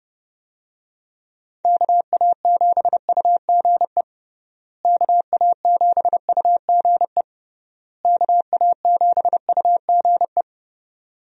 Morse code, heard as a keyed tone, about 32 wpm, 700 Hz. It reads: KA7UGE KA7UGE KA7UGE
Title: USA Call Signs - 6 Characters - Repeated 3x - 30wpm